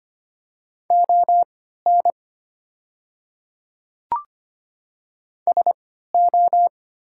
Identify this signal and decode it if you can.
Morse code, heard as a keyed tone, about 25 wpm, spaced as for 14 wpm, 700 Hz. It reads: ON SO
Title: 3x Character Spacing - Top 100 Words - 25wpm